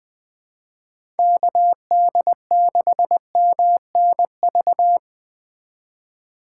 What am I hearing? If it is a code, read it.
Morse code, heard as a keyed tone, about 20 wpm, 700 Hz. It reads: KD6MNV
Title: USA Call Signs - 6 Characters - 2x Character Spacing - 20wpm